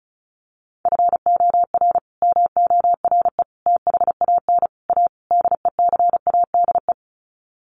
Morse code, heard as a keyed tone, about 34 wpm, 700 Hz. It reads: FOR MORE THAN A DECADE
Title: Sets of 5 Words 35wpm